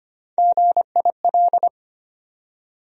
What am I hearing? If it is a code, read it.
Morse code, heard as a keyed tone, about 25 wpm, 700 Hz. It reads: GIL
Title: USA Call Sign Postfixes - 3 Letters 25wpm